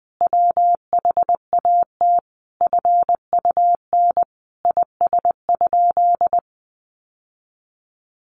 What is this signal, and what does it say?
Morse code, heard as a keyed tone, about 20 wpm, 700 Hz. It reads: WHAT FUN IS?
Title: Sentences from Top 700 Words 20wpm